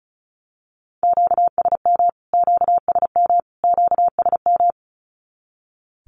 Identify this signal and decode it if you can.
Morse code, heard as a keyed tone, about 34 wpm, 700 Hz. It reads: QSM QSM QSM
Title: Q-Codes Repeated 3x 35wpm